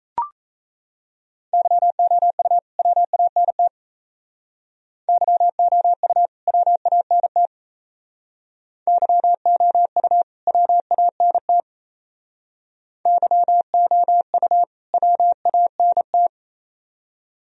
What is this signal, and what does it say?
Morse code, heard as a keyed tone, about 33 wpm, 700 Hz. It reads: YOU WANT YOU WANT YOU WANT YOU WANT
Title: Speed-Racing - Sets of 2 Words 28wpm